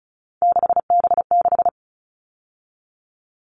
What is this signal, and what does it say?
Morse code, heard as a keyed tone, about 34 wpm, 700 Hz. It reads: 6B6
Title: All Cut Numbers - 3-Digit 35wpm